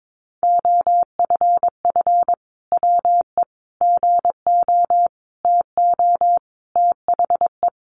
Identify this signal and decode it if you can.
Morse code, heard as a keyed tone, about 22 wpm, 700 Hz. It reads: OFF WE GO TO THE
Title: Sentences from Top 1400 Words 22wpm